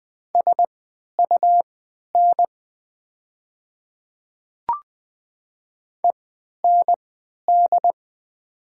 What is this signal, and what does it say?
Morse code, heard as a keyed tone, about 20 wpm, spaced as for 11 wpm, 700 Hz. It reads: SUN END
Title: 3x Character Spacing - Top 300 Words - 20wpm